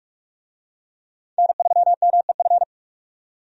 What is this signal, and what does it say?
Morse code, heard as a keyed tone, about 45 wpm, 700 Hz. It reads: N3MEF
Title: USA Call Signs - 5 Characters - 45wpm